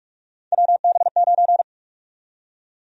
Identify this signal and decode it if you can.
Morse code, heard as a keyed tone, about 46 wpm, 700 Hz. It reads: WB9
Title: USA Call Sign Prefixes - 45wpm